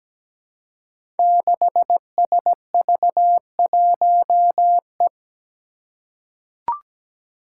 Morse code, heard as a keyed tone, about 17 wpm, 700 Hz. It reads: 6SV1E
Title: Five Challenging Letter-Number Combinations 17wpm